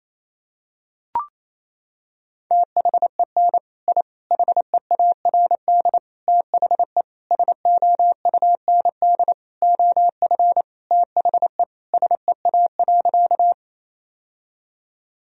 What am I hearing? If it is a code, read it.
Morse code, heard as a keyed tone, about 28 wpm, 700 Hz. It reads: THEN I HEARD THE SOUND OF THE SEA.
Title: Sentences from Top 1000 Words - Review Part 1 - 28wpm